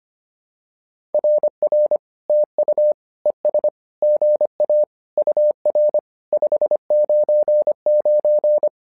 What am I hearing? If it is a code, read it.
Morse code, heard as a keyed tone, about 25 wpm, 600 Hz. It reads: RR TU ES GA UR 599
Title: SOTA QSO Practice 25wpm